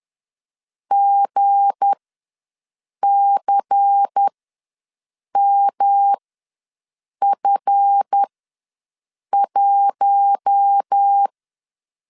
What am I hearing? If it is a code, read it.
Morse code, heard as a keyed tone, about 10 wpm, spaced as for 6 wpm, 800 Hz. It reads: GCMF1
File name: LEZIONI / 7 GF9Z / 7 GF ws 10wpm 3918